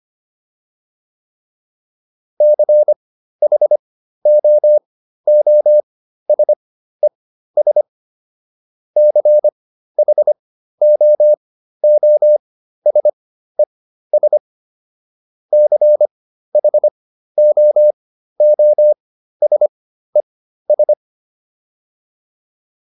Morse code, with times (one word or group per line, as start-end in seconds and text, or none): 2.4-7.8 CHOOSES
9.0-14.4 CHOOSES
15.5-20.9 CHOOSES